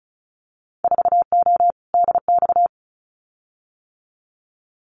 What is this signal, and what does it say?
Morse code, heard as a keyed tone, about 34 wpm, 700 Hz. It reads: 4O DX
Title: Winter Field Day Exchanges Encore 35wpm